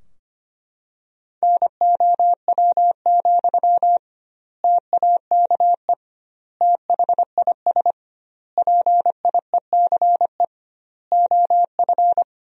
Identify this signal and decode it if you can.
Morse code, heard as a keyed tone, about 25 wpm, 700 Hz. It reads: NOW, TAKE THIS PIECE OF
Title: 2x Word-Spacing - Sentences from Top 400 Words - 25wpm